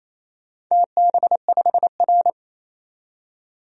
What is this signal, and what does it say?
Morse code, heard as a keyed tone, about 28 wpm, 700 Hz. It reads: TB5R
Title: DXCC-100 Call Signs - 4 Characters - 28wpm